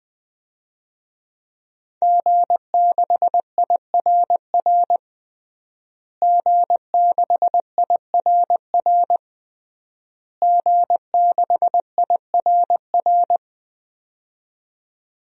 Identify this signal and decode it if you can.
Morse code, heard as a keyed tone, about 20 wpm, 700 Hz. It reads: G6IRR G6IRR G6IRR
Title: DXCC-100 Call Signs - 5 Characters - Repeated 3x - 20wpm